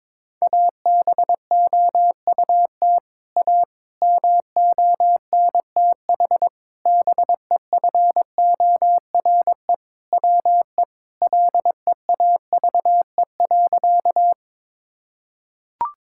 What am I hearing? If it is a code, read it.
Morse code, heard as a keyed tone, about 21 wpm, 700 Hz. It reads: ABOUT A MONTH BEFORE WE LEAVE.
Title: Sentences from Top 1000 Words - Review Part 5 - 22wpm